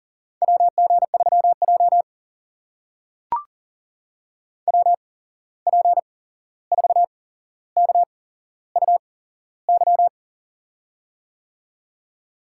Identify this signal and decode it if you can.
Morse code, heard as a keyed tone, about 40 wpm, spaced as for 15 wpm, 700 Hz. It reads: WG3J WP4KUY
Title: USA Call Signs - 8x Character Spacing - 40wpm